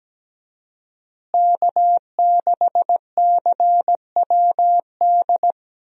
Morse code, heard as a keyed tone, about 17 wpm, 700 Hz. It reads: K6CWD
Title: USA Call Signs - 5 Characters - Repeated 2x - 17wpm